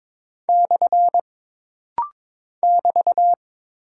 Morse code, heard as a keyed tone, about 22 wpm, 700 Hz. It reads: / <BT>
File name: practice / 22wpm / Single Punctuation-Prosign - Rapid-Fire - 22wpm